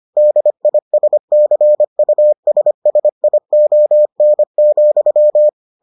Morse code, heard as a keyed tone, about 25 wpm, 600 Hz. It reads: DISCUSSION,